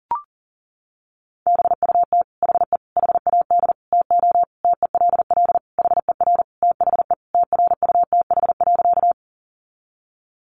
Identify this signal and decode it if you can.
Morse code, heard as a keyed tone, about 37 wpm, 700 Hz. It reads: BUT HE HAD TO TELL HER THE TRUTH.
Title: Sentences from Top 800 Words 40wpm